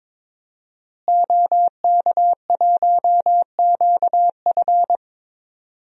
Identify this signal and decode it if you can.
Morse code, heard as a keyed tone, about 21 wpm, 700 Hz. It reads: OK1QF
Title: DXCC-100 Call Signs - 5 Characters - Repeated 2x - 22wpm